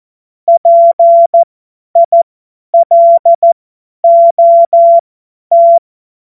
Morse code, heard as a keyed tone, about 14 wpm, spaced as for 10 wpm, 680 Hz. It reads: PILOT
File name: Morsekurs / Witze / Witz 06 Februar 2024 50 F70